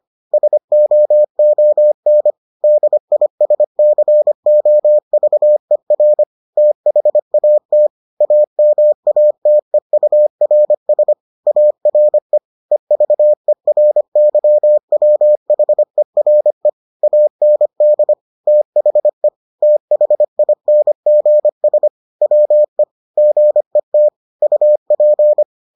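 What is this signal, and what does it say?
Morse code, heard as a keyed tone, about 25 wpm, 600 Hz. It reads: SOON DISCOVER THAT AMATEURS ARE EVERYWHERE AND THE THINGS WE GET UP
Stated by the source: Morse edition of the Foundations of Amateur Radio episodes.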